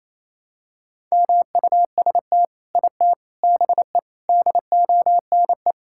Morse code, heard as a keyed tone, about 27 wpm, 700 Hz. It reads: MUST IT BE DONE
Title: Sentences from Top 1000 Words - Review Part 5 - 28wpm